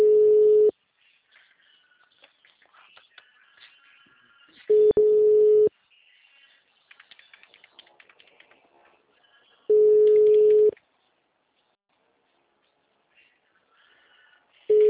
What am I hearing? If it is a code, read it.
Ringback tone (Europe, 425 Hz).
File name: out- -2018- - - .17817